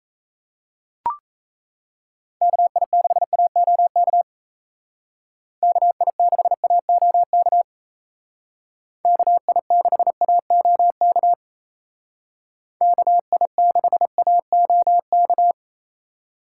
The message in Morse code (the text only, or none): KI6AOK KI6AOK KI6AOK KI6AOK